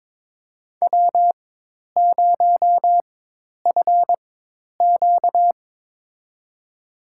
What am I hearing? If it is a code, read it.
Morse code, heard as a keyed tone, about 22 wpm, spaced as for 10 wpm, 700 Hz. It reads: W0FQ
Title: USA Call Signs - 4x Character Spacing - 22wpm